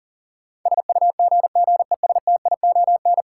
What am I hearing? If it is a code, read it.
Morse code, heard as a keyed tone, about 40 wpm, 700 Hz. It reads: SUGGESTION
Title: Sentences from Top 1600 Words - New Word Drill Alt - 40wpm